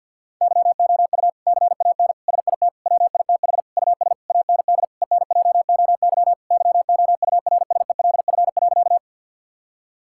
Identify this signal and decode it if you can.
Morse code, heard as a keyed tone, about 50 wpm, 700 Hz. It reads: YOU CAN SIT WITH US AND ENJOY YOURSELF.
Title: Sentences from Top 1000 Words - Review Part 5 - 50wpm